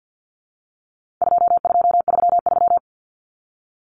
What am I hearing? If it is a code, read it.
Morse code, heard as a keyed tone, about 50 wpm, 700 Hz. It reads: 2233
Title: Common Cut Numbers - 4-Digit 50wpm